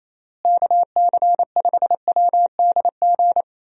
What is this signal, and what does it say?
Morse code, heard as a keyed tone, about 27 wpm, 700 Hz. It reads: KC5WDG
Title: USA Call Signs - 6 Characters - Repeated 3x - 28wpm